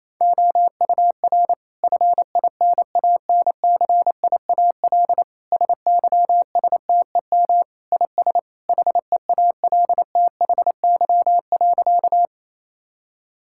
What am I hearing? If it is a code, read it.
Morse code, heard as a keyed tone, about 28 wpm, 700 Hz. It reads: OUR FINANCIAL SYSTEM IS HEALTHY.
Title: Sentences from Top 1000 Words - Review Part 5 - 28wpm